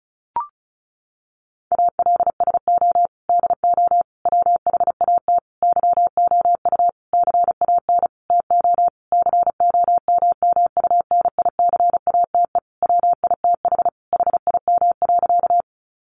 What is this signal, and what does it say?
Morse code, heard as a keyed tone, about 34 wpm, 700 Hz. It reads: ALSO DO WHAT YOU CAN TO COMMUNICATE WITH HIM.
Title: Sentences from Top 1300 Words 35wpm